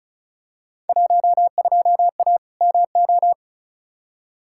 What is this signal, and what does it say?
Morse code, heard as a keyed tone, about 34 wpm, 700 Hz. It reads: 12A MO
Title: Field Day Exchanges Encore - 35wpm